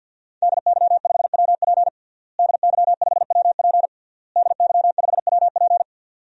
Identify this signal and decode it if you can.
Morse code, heard as a keyed tone, about 47 wpm, 700 Hz. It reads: DY5WP DY5WP DY5WP
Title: DXCC-100 Call Signs - 5 Characters - Repeated 3x - 50wpm